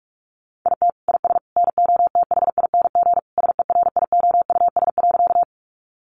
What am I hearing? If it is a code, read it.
Morse code, heard as a keyed tone, about 46 wpm, 700 Hz. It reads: IT IS NOTHING SERIOUS.
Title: Sentences from Top 1100 Words 45wpm